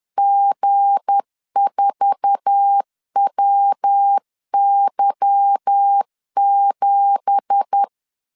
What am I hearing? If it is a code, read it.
Morse code, heard as a keyed tone, about 10 wpm, 800 Hz. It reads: G4WY7